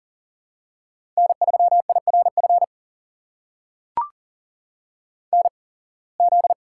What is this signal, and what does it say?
Morse code, heard as a keyed tone, about 40 wpm, spaced as for 26 wpm, 700 Hz. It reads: N3IRF NZ
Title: USA Call Signs - 8x Character Spacing - 40wpm